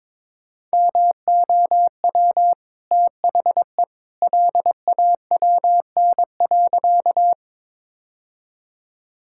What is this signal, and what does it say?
Morse code, heard as a keyed tone, about 22 wpm, 690 Hz. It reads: MOW THE LAWN.
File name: practice / 22wpm / Easy Sentences with Hints - 22wpm